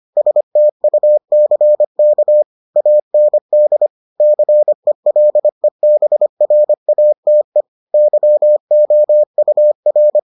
Morse code, heard as a keyed tone, about 25 wpm, 600 Hz. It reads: STUCK AND CELEBRATE YOUR